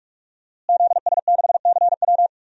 Morse code, heard as a keyed tone, about 46 wpm, 700 Hz. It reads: ZS6CW
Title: DXCC-100 Call Signs - 5 Characters - Repeated 2x - 45wpm